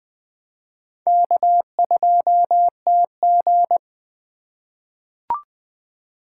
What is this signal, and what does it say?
Morse code, heard as a keyed tone, about 20 wpm, 700 Hz. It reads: K2TG
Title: USA Call Signs - 4 Characters - 3x Character Spacing - 20wpm